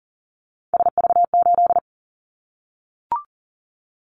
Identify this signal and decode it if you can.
Morse code, heard as a keyed tone, about 41 wpm, 700 Hz. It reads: SV8
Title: Callsign Prefixes for Top 50 DXCC - 40wpm